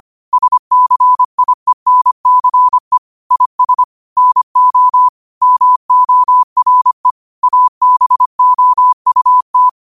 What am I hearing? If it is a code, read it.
Morse code, heard as a keyed tone, about 25 wpm, 1000 Hz. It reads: SCIENCE IS NO MORE ABOUT